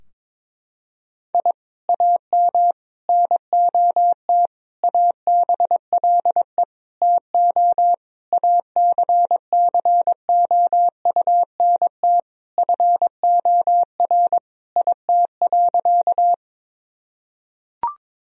Morse code, heard as a keyed tone, about 22 wpm, 700 Hz. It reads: I AM NOT ABLE TO ACCOUNT FOR IT.
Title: Sentences from Top 1000 Words - Review Part 2 - 22wpm